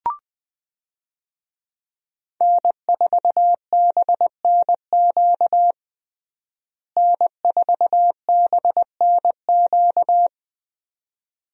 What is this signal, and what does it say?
Morse code, heard as a keyed tone, about 20 wpm, 700 Hz. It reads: N4BNQ N4BNQ
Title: USA Call Signs - 5 Characters - Repeated 2x - 20wpm